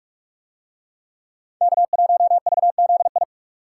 Morse code, heard as a keyed tone, about 45 wpm, 700 Hz. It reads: K1VZI